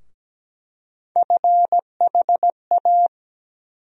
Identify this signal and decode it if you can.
Morse code, heard as a keyed tone, about 17 wpm, 700 Hz. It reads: FHA
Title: USA Call Sign Postfixes - 3 Letters 17wpm